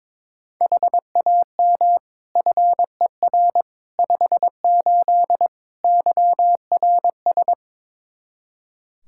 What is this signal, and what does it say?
Morse code, heard as a keyed tone, about 22 wpm, 700 Hz. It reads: HAM FER 58 YRS
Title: K1USN SST QSO Practice 22wpm